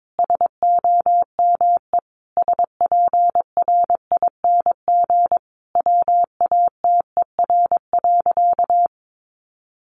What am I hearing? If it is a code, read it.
Morse code, heard as a keyed tone, about 22 wpm, 700 Hz. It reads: SOME SPRING WATER.
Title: Sentences from Top 1200 Words 22wpm